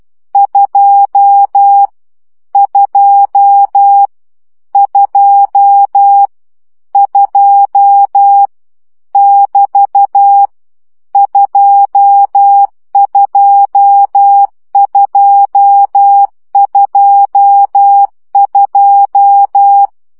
Morse code, heard as a keyed tone, about 12 wpm, 800 Hz. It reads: 2 2 2 2 <BT> 22222
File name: CW gyakorlas 2 10 baud